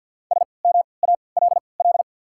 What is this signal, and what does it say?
Morse code, heard as a keyed tone, about 53 wpm, spaced as for 29 wpm, 690 Hz. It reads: SMALL
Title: 3x Character Spacing - Top 200 Words - 50wpm